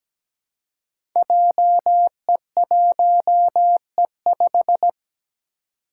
Morse code, heard as a keyed tone, about 17 wpm, 700 Hz. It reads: JE1E5
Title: Five Challenging Letter-Number Combinations 17wpm